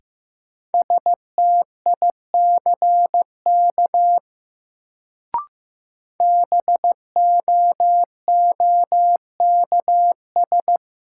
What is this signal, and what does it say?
Morse code, heard as a keyed tone, about 15 wpm, 700 Hz. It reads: STICK BOOKS
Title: Sentences from Top 1200 Words - New Word Drill Alt - 15wpm